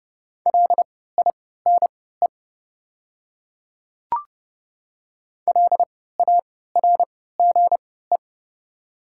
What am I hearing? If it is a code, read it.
Morse code, heard as a keyed tone, about 31 wpm, spaced as for 17 wpm, 700 Hz. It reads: LINE LARGE
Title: 3x Character Spacing - Top 300 Words - 30wpm